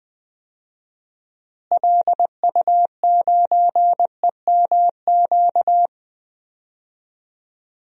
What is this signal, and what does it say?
Morse code, heard as a keyed tone, about 20 wpm, 700 Hz. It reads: LU9EMQ
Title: DXCC-100 Call Signs - 6 Characters - Repeated 2x - 20wpm